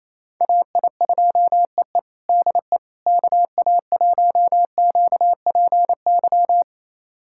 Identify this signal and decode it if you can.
Morse code, heard as a keyed tone, about 28 wpm, 700 Hz. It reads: AI2EE DE KA1QPY
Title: USA Callsign DE Callsign 28wpm